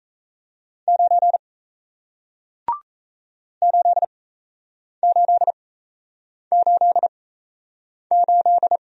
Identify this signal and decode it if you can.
Morse code, heard as a keyed tone, about 38 wpm, spaced as for 8 wpm, 700 Hz. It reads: 9 8888